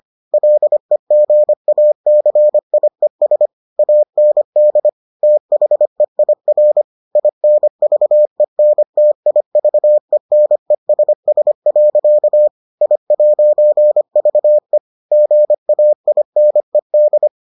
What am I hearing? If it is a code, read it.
Morse code, heard as a keyed tone, about 25 wpm, 600 Hz. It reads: LEGACIES AND THEIR INVENTIVENESS. I'VE GAINED